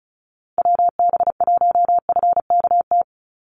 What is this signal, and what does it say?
Morse code, heard as a keyed tone, about 34 wpm, 700 Hz. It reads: WB1FKT
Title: USA Call Signs 35wpm